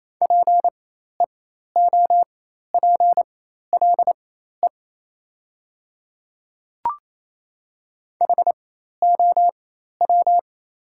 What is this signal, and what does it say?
Morse code, heard as a keyed tone, about 28 wpm, spaced as for 13 wpm, 700 Hz. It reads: PEOPLE HOW
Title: 4x Character Spacing - Top 100 Words - 28wpm